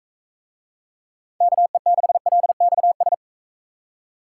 Morse code, heard as a keyed tone, about 39 wpm, 700 Hz. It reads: KE6LXS